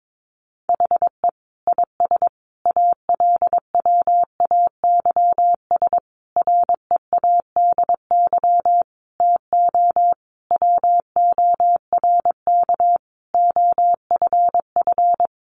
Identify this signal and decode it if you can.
Morse code, heard as a keyed tone, about 22 wpm, 700 Hz. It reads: HE IS ALWAYS READY TO WORK OFF
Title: Sentences from Top 1300 Words 22wpm